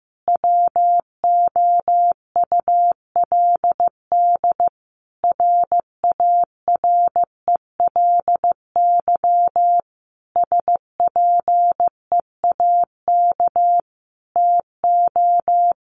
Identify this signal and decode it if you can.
Morse code, heard as a keyed tone, about 15 wpm, 700 Hz. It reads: WOULD RARELY SPEAK TO
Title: Sentences from Top 1400 Words 15wpm